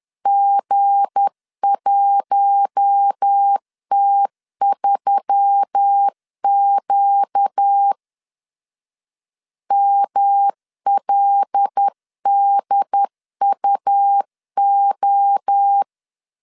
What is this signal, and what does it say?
Morse code, heard as a keyed tone, about 10 wpm, 800 Hz. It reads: G1T3Q MLDUO
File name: LEZIONI / 7 GF9Z / 7 GF 10wpm 2514